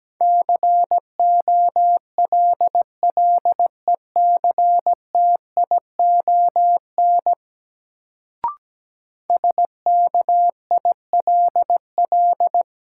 Morse code, heard as a keyed tone, about 17 wpm, 700 Hz. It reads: COLLECTION SKILL
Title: Sentences from Top 1200 Words - New Word Drill - 17wpm